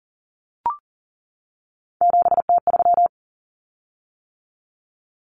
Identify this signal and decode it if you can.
Morse code, heard as a keyed tone, about 40 wpm, 700 Hz. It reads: 7T3
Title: Common Cut Numbers - 3-Digit 40wpm